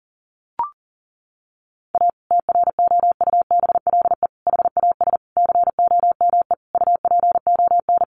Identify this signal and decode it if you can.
Morse code, heard as a keyed tone, about 37 wpm, 700 Hz. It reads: A TROUBLE HAS COME UPON
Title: Sentences from Top 1200 Words 40wpm